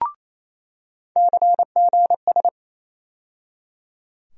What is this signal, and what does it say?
Morse code, heard as a keyed tone, about 27 wpm, 700 Hz. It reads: CGS
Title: USA Call Sign Postfixes - 3 Letters 28wpm